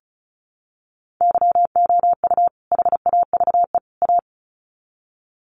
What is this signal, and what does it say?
Morse code, heard as a keyed tone, about 34 wpm, 700 Hz. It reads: YOU HAVE A
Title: Sets of 3 Words 35wpm